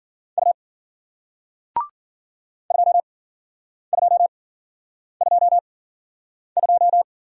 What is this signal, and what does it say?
Morse code, heard as a keyed tone, about 56 wpm, spaced as for 9 wpm, 700 Hz. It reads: U 2222